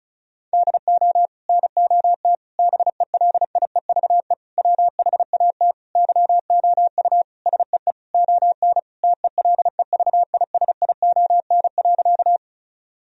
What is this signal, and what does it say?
Morse code, heard as a keyed tone, about 34 wpm, 700 Hz. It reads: DO NOT BELIEVE WHAT YOU SEE ON TELEVISION.
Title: Sentences from Top 1000 Words - Review Part 4 - 35wpm